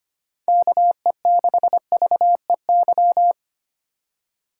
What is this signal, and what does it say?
Morse code, heard as a keyed tone, about 25 wpm, 700 Hz. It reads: KE6VEY
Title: USA Call Signs - 2x Character Spacing - 25wpm